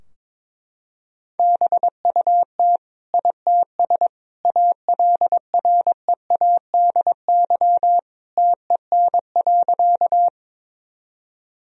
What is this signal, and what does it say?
Morse code, heard as a keyed tone, about 22 wpm, 700 Hz. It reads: BUT ITS ALREADY TEN.
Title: Sentences from Top 1000 Words - Review Part 2 - 22wpm